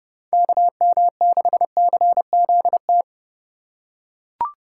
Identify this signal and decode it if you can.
Morse code, heard as a keyed tone, about 30 wpm, 700 Hz. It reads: KM6CZT